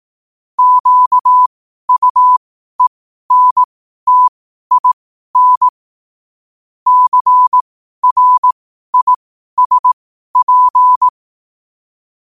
Morse code, heard as a keyed tone, about 18 wpm, spaced as for 12 wpm, 1000 Hz. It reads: QUENTIN CRISP
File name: qotdfiles / 12wpm